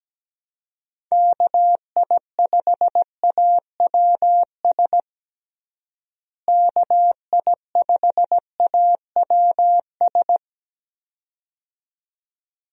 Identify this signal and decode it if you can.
Morse code, heard as a keyed tone, about 17 wpm, 700 Hz. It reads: KI5AWS KI5AWS
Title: USA Call Signs - 6 Characters - Repeated 2x - 17wpm